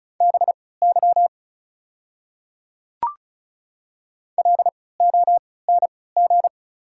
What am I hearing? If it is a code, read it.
Morse code, heard as a keyed tone, about 34 wpm, spaced as for 20 wpm, 690 Hz. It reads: BY LONG
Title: 3x Character Spacing - Top 100 Words - 35wpm